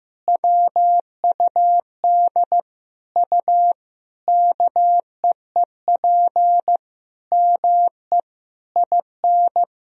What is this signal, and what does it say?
Morse code, heard as a keyed tone, about 15 wpm, 700 Hz. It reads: WUD U KEEP ME IN
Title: Sentences from Top 500 Words and Abbreviations 15wpm